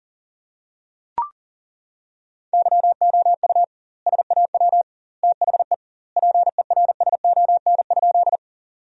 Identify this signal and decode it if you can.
Morse code, heard as a keyed tone, about 40 wpm, 700 Hz. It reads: YOU SAW THE PERSON?
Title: 2x Word-Spacing - Sentences from Top 400 Words - 40wpm